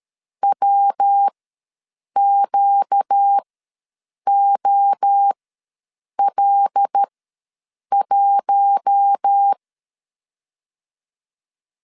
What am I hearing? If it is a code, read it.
Morse code, heard as a keyed tone, about 13 wpm, spaced as for 7 wpm, 800 Hz. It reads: WQOL1
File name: LEZIONI / 5 LBVC / 5 L ws 12wpm 2958